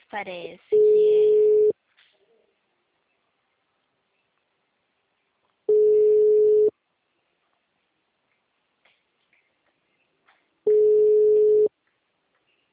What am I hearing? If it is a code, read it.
Ringback tone (Europe, 425 Hz), with speech.
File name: out- -2005- - - .37